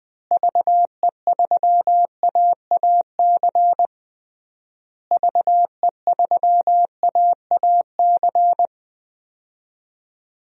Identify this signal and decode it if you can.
Morse code, heard as a keyed tone, about 20 wpm, 700 Hz. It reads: VE3AAC VE3AAC
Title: DXCC-100 Call Signs - 6 Characters - Repeated 3x - 20wpm